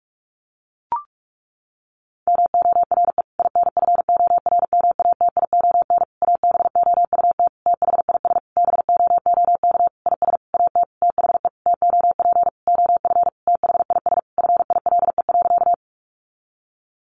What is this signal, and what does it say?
Morse code, heard as a keyed tone, about 44 wpm, 700 Hz. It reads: MORE INFORMATION ABOUT THIS BOOK IS AT THE TOP OF THIS FILE.
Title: Sentences from Top 1200 Words 45wpm